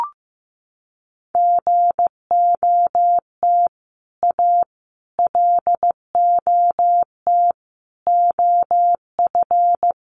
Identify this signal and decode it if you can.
Morse code, heard as a keyed tone, about 15 wpm, 700 Hz. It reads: GOT A LOT OF
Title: Sets of 4 Words 15wpm